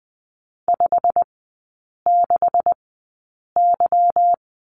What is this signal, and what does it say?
Morse code, heard as a keyed tone, about 20 wpm, spaced as for 8 wpm, 700 Hz. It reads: 56Y
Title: 152 Lesson 25e - Letter p - Three Characters 20wpm